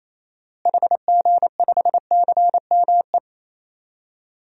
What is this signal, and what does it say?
Morse code, heard as a keyed tone, about 28 wpm, 700 Hz. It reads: HG5CME
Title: DXCC-100 Call Signs - 6 Characters - Repeated 3x - 28wpm